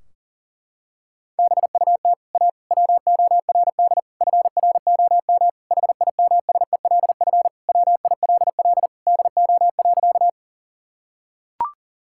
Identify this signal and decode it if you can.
Morse code, heard as a keyed tone, about 40 wpm, 700 Hz. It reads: BUT A WORD FROM HIMSELF WILL DO.
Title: Sentences from Top 1600 Words 40wpm